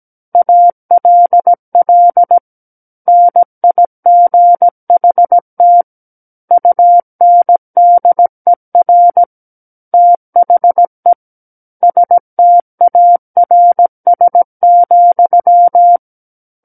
Morse code, heard as a keyed tone, about 17 wpm, 700 Hz. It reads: ALL NIGHT UNDER THE STARS,
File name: Book One / 08-Chapter 8 - 17 WPM